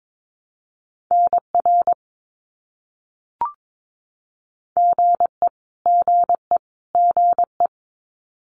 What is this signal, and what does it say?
Morse code, heard as a keyed tone, about 22 wpm, 700 Hz. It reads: NR GE GE GE
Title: Abbreviations Repeated 3x - 22wpm